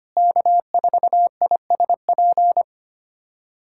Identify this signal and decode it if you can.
Morse code, heard as a keyed tone, about 25 wpm, 700 Hz. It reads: K4ISP
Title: USA Callsign DE Callsign - 5 Characters - 25wpm